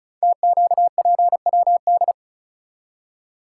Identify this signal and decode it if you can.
Morse code, heard as a keyed tone, about 33 wpm, 700 Hz. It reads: TQPWD
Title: Code Groups - 5 Letters - 35wpm